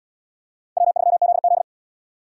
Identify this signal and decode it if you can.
Morse code, heard as a keyed tone, about 68 wpm, 700 Hz. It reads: W3ZZ